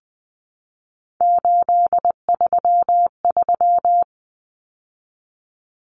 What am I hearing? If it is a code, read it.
Morse code, heard as a keyed tone, about 20 wpm, 700 Hz. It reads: 833
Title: Common Cut Numbers - 3-Digit 20wpm